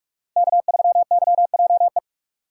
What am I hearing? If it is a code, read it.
Morse code, heard as a keyed tone, about 46 wpm, 700 Hz. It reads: K3YJE
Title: USA Call Signs - 5 Characters - 6x Character Spacing - 45wpm